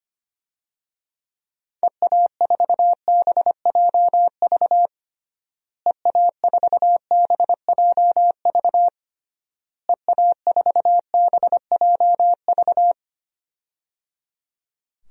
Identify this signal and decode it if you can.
Morse code, heard as a keyed tone, about 25 wpm, 700 Hz. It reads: EA4BJV EA4BJV EA4BJV
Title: DXCC-100 Call Signs - 6 Characters - Repeated 3x - 25wpm